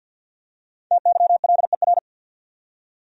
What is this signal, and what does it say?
Morse code, heard as a keyed tone, about 52 wpm, 700 Hz. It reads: TYLER
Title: USA Common Names - Max 5 Letters - 50wpm